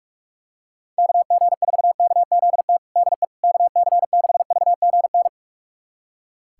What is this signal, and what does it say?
Morse code, heard as a keyed tone, about 46 wpm, 700 Hz. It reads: KG4KZT DE KC6VGN
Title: USA Callsign DE Callsign 45wpm